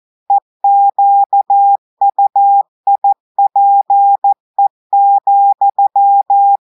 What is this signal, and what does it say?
Morse code, heard as a keyed tone, about 14 wpm, 800 Hz. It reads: EQUIPE,
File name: audios / portugais / portugais texte en clair 2 vitesse 14 mots minutes